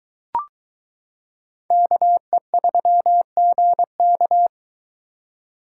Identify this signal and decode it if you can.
Morse code, heard as a keyed tone, about 22 wpm, 700 Hz. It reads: KE3GK